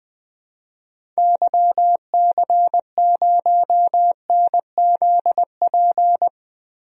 Morse code, heard as a keyed tone, about 21 wpm, 700 Hz. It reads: YC0NZP
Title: DXCC-100 Call Signs - 6 Characters - Repeated 2x - 20wpm